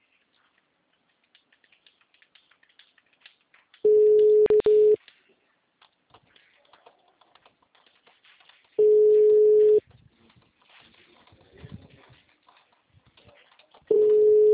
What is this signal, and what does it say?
Ringback tone (Europe, 425 Hz).
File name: out- -2013- - - .70004